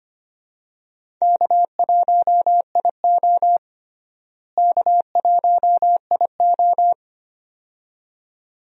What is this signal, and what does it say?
Morse code, heard as a keyed tone, about 25 wpm, 700 Hz. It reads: K1IO K1IO
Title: USA Call Signs - 4 Characters - Repeated 2x - 25wpm